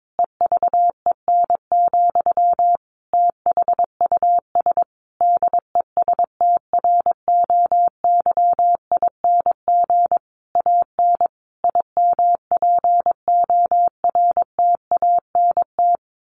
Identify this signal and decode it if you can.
Morse code, heard as a keyed tone, about 22 wpm, 700 Hz. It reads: EVEN, THUS DESTROYING AN IMPORTANT
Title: A Sound Of Thunder 22wpm